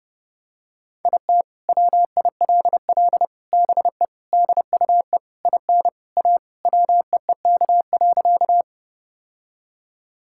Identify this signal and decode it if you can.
Morse code, heard as a keyed tone, about 30 wpm, 700 Hz. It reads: IT WILL BE DUE IN A WEEK.
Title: Sentences from Top 1000 Words - Review Part 6 - 30wpm